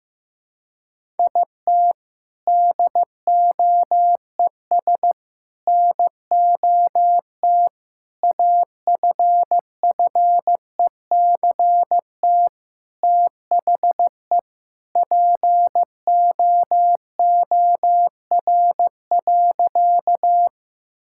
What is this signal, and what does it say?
Morse code, heard as a keyed tone, about 15 wpm, 700 Hz. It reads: IT DOES NOT AFFECT THE POOR.
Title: Sentences from Top 1000 Words - Review Part 6 - 15wpm